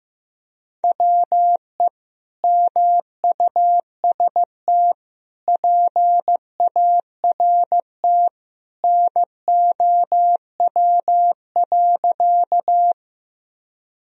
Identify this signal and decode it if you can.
Morse code, heard as a keyed tone, about 15 wpm, 700 Hz. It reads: WE MUST PART NOW.
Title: Sentences from Top 1000 Words - Review Part 1 - 15wpm